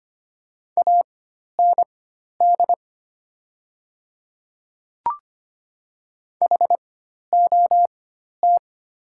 Morse code, heard as a keyed tone, about 25 wpm, spaced as for 12 wpm, 700 Hz. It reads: AND HOT
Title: 4x Character Spacing - Top 500 Words - 25wpm